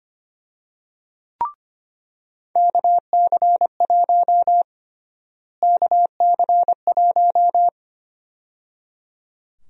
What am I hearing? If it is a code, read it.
Morse code, heard as a keyed tone, about 25 wpm, 700 Hz. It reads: KC1 KC1
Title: USA Call Sign Prefixes - Repeated 2x - 25wpm